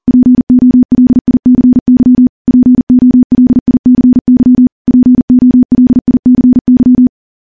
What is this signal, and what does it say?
Morse code, heard as a keyed tone, about 40 wpm, 250 Hz. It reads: POLICY POLICY POLICY
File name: morse-training-40WPM